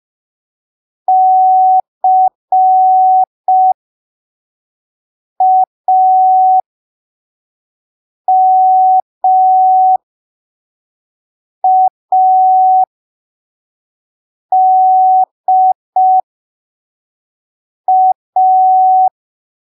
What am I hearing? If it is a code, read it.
Morse code, heard as a keyed tone, about 5 wpm, spaced as for 3 wpm, 750 Hz. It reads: CAMADA
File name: arquivos / texto / texto 10 05ppm